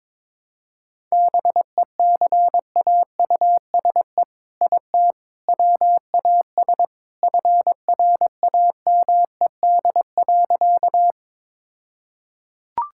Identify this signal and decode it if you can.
Morse code, heard as a keyed tone, about 22 wpm, 700 Hz. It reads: BECAUSE IT WAS FRAMED.
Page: morse-code-ninja-practice-22wpm